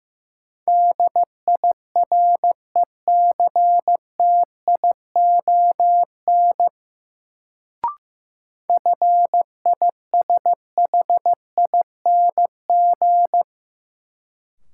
Morse code, heard as a keyed tone, about 15 wpm, 700 Hz. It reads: DIRECTION FISHING
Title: Sentences from Top 1000 Words - Review Part 5 - New Word Drill - 15wpm